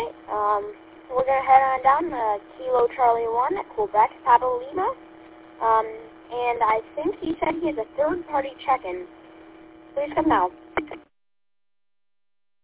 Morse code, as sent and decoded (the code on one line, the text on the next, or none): none